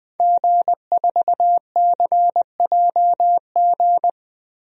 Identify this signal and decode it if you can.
Morse code, heard as a keyed tone, about 20 wpm, 700 Hz. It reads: G4CJG
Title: DXCC-100 Call Signs - 5 Characters - Repeated 2x - 20wpm